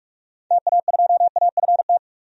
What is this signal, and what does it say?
Morse code, heard as a keyed tone, about 46 wpm, 700 Hz. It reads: TA2AFT
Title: DXCC-100 Call Signs - 6 Characters - 45wpm